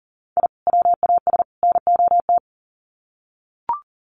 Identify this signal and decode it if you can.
Morse code, heard as a keyed tone, about 40 wpm, 700 Hz. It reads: I WAS NOT
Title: Sets of 3 Words 40wpm